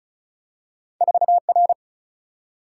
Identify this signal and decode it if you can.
Morse code, heard as a keyed tone, about 36 wpm, 700 Hz. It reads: 4 R
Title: Single-Letter Single-Number 35wpm